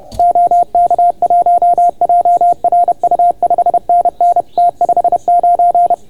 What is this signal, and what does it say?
Morse code, heard as a keyed tone, about 29 wpm, 690 Hz. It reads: OK1JRU 5NN T59